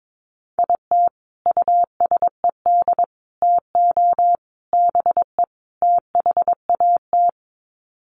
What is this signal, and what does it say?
Morse code, heard as a keyed tone, about 22 wpm, 700 Hz. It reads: IT USED TO BE THAT
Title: Sets of 5 Words Encore2 22wpm